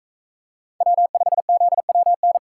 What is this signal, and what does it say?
Morse code, heard as a keyed tone, about 41 wpm, 700 Hz. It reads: W5ZWN